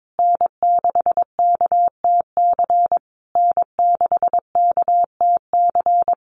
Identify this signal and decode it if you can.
Morse code, heard as a keyed tone, about 22 wpm, 700 Hz. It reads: N6KTC N6KTC
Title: USA Call Signs Repeated 3x 22wpm